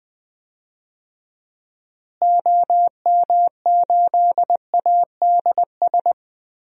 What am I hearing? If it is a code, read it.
Morse code, heard as a keyed tone, about 20 wpm, 700 Hz. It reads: OM8ADS
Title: DXCC-100 Call Signs - 6 Characters - Repeated 3x - 20wpm